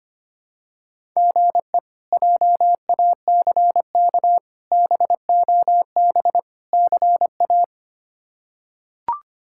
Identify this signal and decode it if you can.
Morse code, heard as a keyed tone, about 25 wpm, 700 Hz. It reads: GE JACK BOB CA
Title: K1USN SST QSO Practice - 25wpm